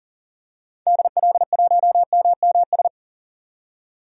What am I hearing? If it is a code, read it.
Morse code, heard as a keyed tone, about 40 wpm, 700 Hz. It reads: DL1MMS
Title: DXCC-100 Call Signs - 6 Characters - 40wpm